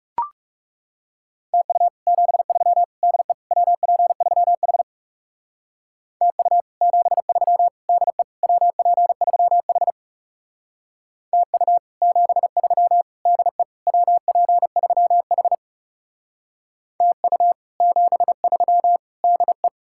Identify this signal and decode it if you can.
Morse code, heard as a keyed tone, about 38 wpm, 700 Hz. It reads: TU 73 DE WP3H TU 73 DE WP3H TU 73 DE WP3H TU 73 DE